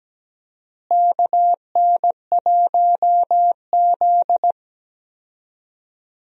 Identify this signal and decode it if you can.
Morse code, heard as a keyed tone, about 17 wpm, 700 Hz. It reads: KN1Z
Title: USA Call Signs - 4 Characters - Repeated 3x - 17wpm